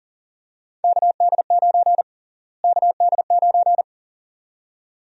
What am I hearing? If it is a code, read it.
Morse code, heard as a keyed tone, about 39 wpm, 700 Hz. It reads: KD9 KD9
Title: USA Call Sign Prefixes - Repeated 2x - 40wpm